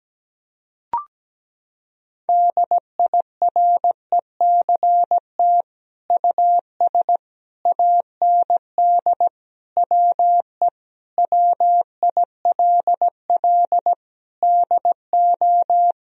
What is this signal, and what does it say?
Morse code, heard as a keyed tone, about 17 wpm, 700 Hz. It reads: DIRECT US AND WE WILL DO
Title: Sentences from Top 1000 Words - Review Part 2 - 17wpm